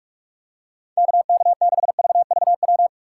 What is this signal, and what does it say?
Morse code, heard as a keyed tone, about 46 wpm, 700 Hz. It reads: KK6VVW
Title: USA Call Signs - 6 Characters - 3x Character Spacing - 45wpm